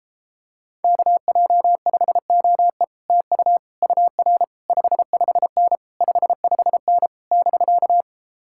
Morse code, heard as a keyed tone, about 29 wpm, 700 Hz. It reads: KJ5OE TU UR 55N 55N <BK>